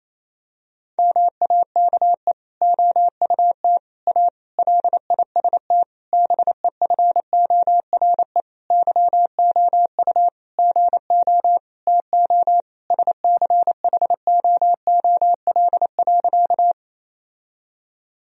Morse code, heard as a keyed tone, about 28 wpm, 700 Hz. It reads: MAKE OUT A LIST BEFORE YOU GO TO SCHOOL.
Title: Sentences from Top 1000 Words - Review Part 2 - 28wpm